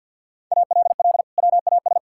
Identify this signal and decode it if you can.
Morse code, heard as a keyed tone, about 53 wpm, 700 Hz. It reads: ALL WAS
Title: Sentences from Top 1000 Words - Review Part 6 - 50wpm